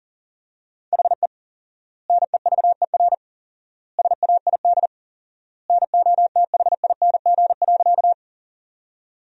Morse code, heard as a keyed tone, about 40 wpm, 700 Hz. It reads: HE NEVER SAID NOTHING.
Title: 4x Word-Spacing - Sentences from Top 500 Words - 40wpm